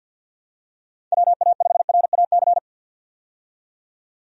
Morse code, heard as a keyed tone, about 51 wpm, 700 Hz. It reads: WA5RAC
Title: USA Call Signs - 6 Characters - 50wpm